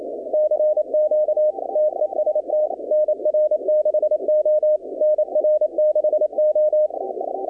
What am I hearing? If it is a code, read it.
Morse code, heard as a keyed tone, about 27 wpm, 610 Hz. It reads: CQ TEST NR6O NR6O
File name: contests / 2019-cw-wpx-kt5j-k5pi- -2330 0-20